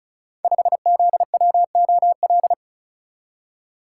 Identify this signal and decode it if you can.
Morse code, heard as a keyed tone, about 36 wpm, 700 Hz. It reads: 5ZWOL
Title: Code Groups - 5 Letter:Numbers - 35wpm